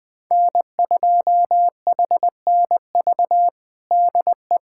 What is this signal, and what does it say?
Morse code, heard as a keyed tone, about 20 wpm, 700 Hz. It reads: N2HNV DE
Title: USA Callsign DE Callsign - 5 Characters - 20wpm